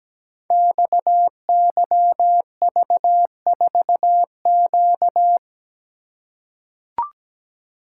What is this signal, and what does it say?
Morse code, heard as a keyed tone, about 17 wpm, 700 Hz. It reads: XYV4Q
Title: Code Groups - 5 Letter:Numbers - 17wpm